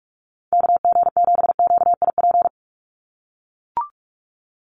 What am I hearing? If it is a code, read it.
Morse code, heard as a keyed tone, about 44 wpm, 700 Hz. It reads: KG7QIP
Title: USA Call Signs 45wpm